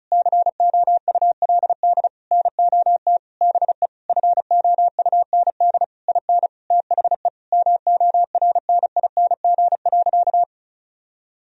Morse code, heard as a keyed tone, about 36 wpm, 700 Hz. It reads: COULD NOT BE FOUND IN THE MORNING.
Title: Sentences from Top 1000 Words - Review Part 1 - 35wpm